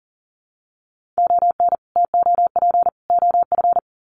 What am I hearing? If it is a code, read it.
Morse code, heard as a keyed tone, about 40 wpm, 700 Hz. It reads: ON TOP OF
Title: Sets of 3 Words 40wpm